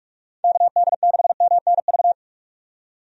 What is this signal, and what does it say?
Morse code, heard as a keyed tone, about 46 wpm, 700 Hz. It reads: KD6MNV
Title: USA Call Signs - 6 Characters - 45wpm